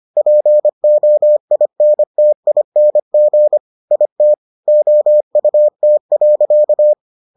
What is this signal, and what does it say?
Morse code, heard as a keyed tone, about 25 wpm, 600 Hz. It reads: POINTING IT OUT.